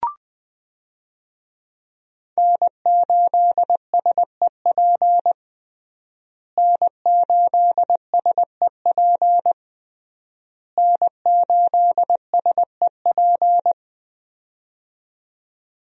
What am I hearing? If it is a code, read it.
Morse code, heard as a keyed tone, about 20 wpm, 700 Hz. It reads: N8SEP N8SEP N8SEP
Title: USA Call Signs - 5 Characters - Repeated 3x - 20wpm